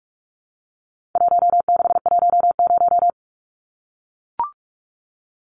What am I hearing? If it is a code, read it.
Morse code, heard as a keyed tone, about 44 wpm, 700 Hz. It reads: 1610
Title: Numbers - 4-Digit 45wpm